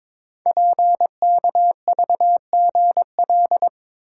Morse code, heard as a keyed tone, about 22 wpm, 700 Hz. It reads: PKVGL
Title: Code Groups - 5 Letters - 22wpm